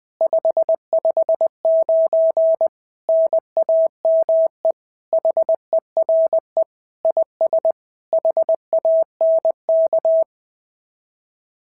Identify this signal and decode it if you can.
Morse code, heard as a keyed tone, about 20 wpm, 650 Hz. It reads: 559 NAME HERE IS HANK
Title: SOTA QSO Practice 20wpm